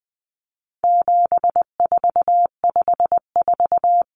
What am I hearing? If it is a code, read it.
Morse code, heard as a keyed tone, about 20 wpm, 700 Hz. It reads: 7454
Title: Numbers - 4-Digit 20wpm